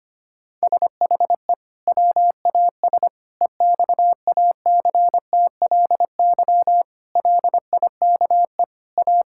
Morse code, heard as a keyed tone, about 25 wpm, 700 Hz. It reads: SHE WAS EXACTLY LIKE A
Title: Sentences from Top 1000 Words - Review Part 6 - 25wpm